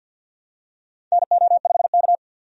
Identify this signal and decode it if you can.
Morse code, heard as a keyed tone, about 52 wpm, 700 Hz. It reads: NO5K
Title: USA Call Signs - 4 Characters - 50wpm